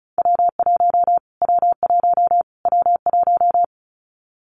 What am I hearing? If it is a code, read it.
Morse code, heard as a keyed tone, about 34 wpm, 700 Hz. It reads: W1 W1 W1
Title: Callsign Prefixes for Top 25 DXCC - Repeated 3x - 35wpm